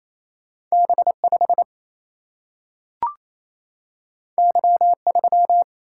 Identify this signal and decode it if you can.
Morse code, heard as a keyed tone, about 28 wpm, 700 Hz. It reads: B 5 Y 3
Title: Single-Letter Single-Number 28wpm